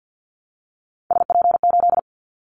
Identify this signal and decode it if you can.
Morse code, heard as a keyed tone, about 50 wpm, 700 Hz. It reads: SP8
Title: Callsign Prefixes for Top 25 DXCC - Repeated 3x - 50wpm